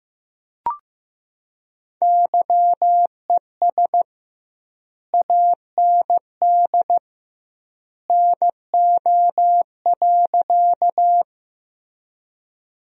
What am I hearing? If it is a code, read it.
Morse code, heard as a keyed tone, about 16 wpm, 700 Hz. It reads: YES AND NO.
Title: 2x Word-Spacing - Sentences from Top 500 Words - 15wpm